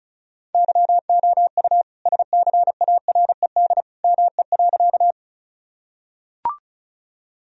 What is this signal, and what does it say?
Morse code, heard as a keyed tone, about 34 wpm, 700 Hz. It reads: YOU SCARED ME.
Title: Sentences from Top 1000 Words - Review Part 4 - 35wpm